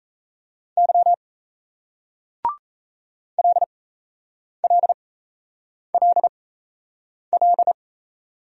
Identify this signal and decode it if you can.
Morse code, heard as a keyed tone, about 38 wpm, spaced as for 8 wpm, 700 Hz. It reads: Y LLLL